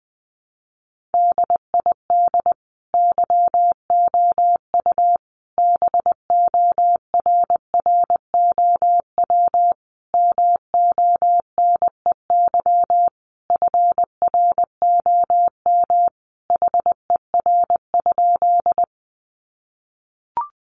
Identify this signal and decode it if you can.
Morse code, heard as a keyed tone, about 19 wpm, 700 Hz. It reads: DID YOU BORROW MONEY FROM HER?
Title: Sentences from Top 1500 Words 20wpm